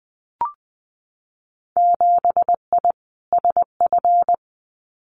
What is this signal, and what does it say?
Morse code, heard as a keyed tone, about 20 wpm, 700 Hz. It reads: 7I SF
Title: Winter Field Day Exchanges Encore 20wpm